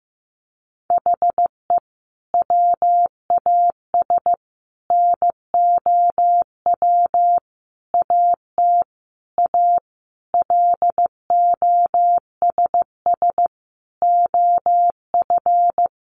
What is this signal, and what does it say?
Morse code, heard as a keyed tone, about 15 wpm, 700 Hz. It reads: HE WAS NOW AT A LOSS OF
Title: Sentences from Top 1400 Words 15wpm